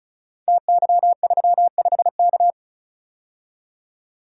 Morse code, heard as a keyed tone, about 36 wpm, 700 Hz. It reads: TY35K
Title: Code Groups - 5 Letter:Numbers - 35wpm